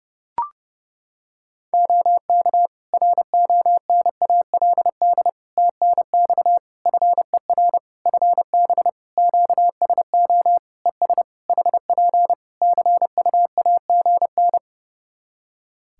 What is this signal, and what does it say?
Morse code, heard as a keyed tone, about 30 wpm, 700 Hz. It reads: OK RONALD TNX FER FB QSO ES HP CUAGN
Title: Practice QSO - Exchange 4 - Ending - 30wpm